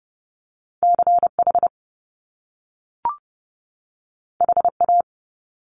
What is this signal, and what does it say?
Morse code, heard as a keyed tone, about 30 wpm, 700 Hz. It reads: CH HA
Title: Top Two Letter Combinations in Words 30wpm